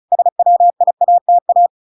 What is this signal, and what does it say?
Morse code, heard as a keyed tone, about 34 wpm, 700 Hz. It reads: SWIATA